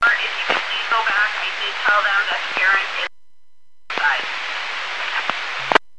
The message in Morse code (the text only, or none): none